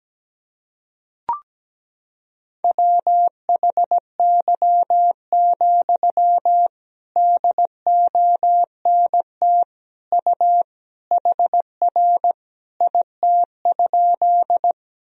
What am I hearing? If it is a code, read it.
Morse code, heard as a keyed tone, about 17 wpm, 700 Hz. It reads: WHY, DONT U HR IT?
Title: Sentences from Top 300 Words and Abbreviations 17wpm